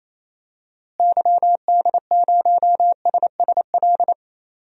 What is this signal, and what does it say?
Morse code, heard as a keyed tone, about 27 wpm, 700 Hz. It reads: YD0SSL
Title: DXCC-100 Call Signs - 6 Characters - Repeated 2x - 28wpm